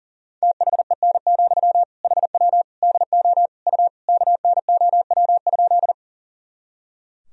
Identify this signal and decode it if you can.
Morse code, heard as a keyed tone, about 40 wpm, 700 Hz. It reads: THEN, HW DO U KNOW?
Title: Sentences from Top 100 Words and Abbreviations 40wpm